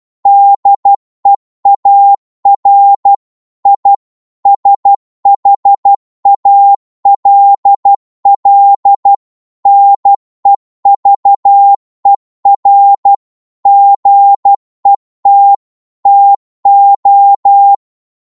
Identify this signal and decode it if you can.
Morse code, heard as a keyed tone, about 12 wpm, 800 Hz. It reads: DEAR I SHALL NEVER GET TO